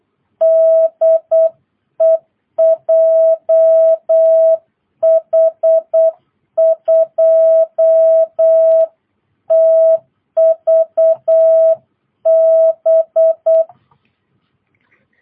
Morse code, heard as a keyed tone, about 8 wpm, 660 Hz. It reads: DE JH2TVB